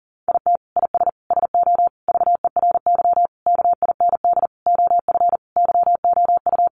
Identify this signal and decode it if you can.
Morse code, heard as a keyed tone, about 40 wpm, 700 Hz. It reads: IT IS SO VERY KIND OF YOU
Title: Sentences from Top 300 Words 40wpm